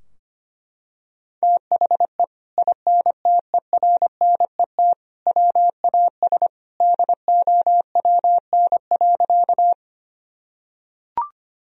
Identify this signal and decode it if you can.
Morse code, heard as a keyed tone, about 25 wpm, 700 Hz. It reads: THE INTERNET WAS DOWN.
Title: Sentences from Top 1000 Words - Review Part 4 - 25wpm